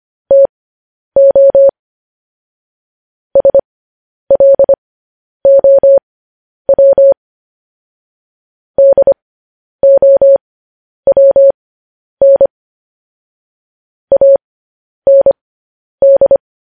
Morse code, heard as a keyed tone, about 25 wpm, spaced as for 10 wpm, 550 Hz. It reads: TO SLOW DOWN AND
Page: news headlines in morse code at 10 wpm